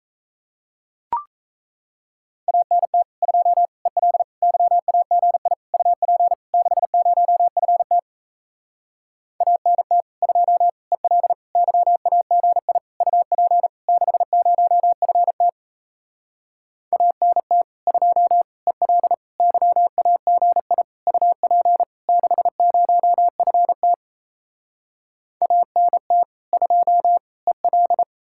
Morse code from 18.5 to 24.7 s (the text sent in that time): EL YAGI UP 60FT